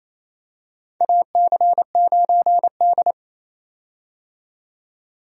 Morse code, heard as a keyed tone, about 28 wpm, 700 Hz. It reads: AC9D
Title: USA Call Signs - 4 Characters - 28wpm